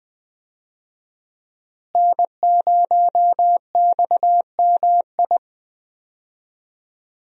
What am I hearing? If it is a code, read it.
Morse code, heard as a keyed tone, about 20 wpm, 700 Hz. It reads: N0XMI
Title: USA Call Signs - 5 Characters - Repeated 2x - 20wpm